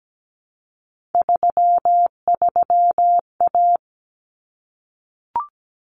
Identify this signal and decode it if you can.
Morse code, heard as a keyed tone, about 17 wpm, 700 Hz. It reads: 33A
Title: Common Cut Numbers - 3-Digit 17wpm